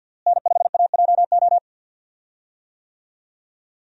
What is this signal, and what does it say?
Morse code, heard as a keyed tone, about 51 wpm, 700 Hz. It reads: N5AJO
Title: USA Call Signs - 5 Characters - Repeated 3x - 50wpm